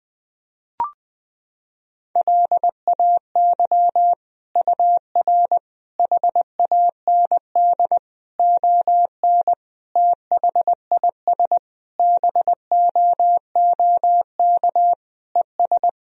Morse code, heard as a keyed tone, about 20 wpm, 700 Hz. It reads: LAY UR HAND ON THIS BOOK ES
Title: Sentences from Top 400 Words and Abbreviations 20wpm